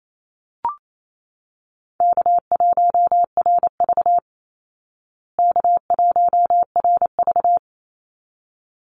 Morse code, heard as a keyed tone, about 28 wpm, 700 Hz. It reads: K1RV K1RV
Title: Call Signs Repeated 3x - 28wpm